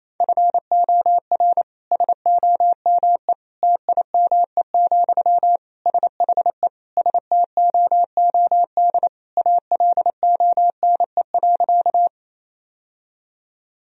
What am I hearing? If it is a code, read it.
Morse code, heard as a keyed tone, about 28 wpm, 700 Hz. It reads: FOR SOME TIME, SHE STOOD ALONE.
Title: Sentences from Top 1000 Words - Review Part 2 - 28wpm